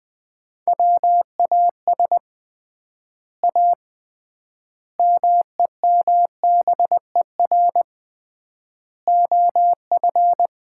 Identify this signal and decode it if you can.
Morse code, heard as a keyed tone, about 21 wpm, 700 Hz. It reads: WAS A MEMBER OF
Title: 3x Word-Spacing - Sets of 5 Words - 20wpm